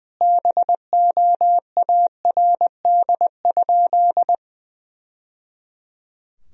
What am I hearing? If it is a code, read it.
Morse code, heard as a keyed tone, about 21 wpm, 700 Hz. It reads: BOARD?
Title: Sentences from Top 1000 Words - Review Part 4 - 20wpm